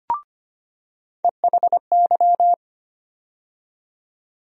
Morse code, heard as a keyed tone, about 25 wpm, 700 Hz. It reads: EHY
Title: USA Call Sign Postfixes - 3 Letters 25wpm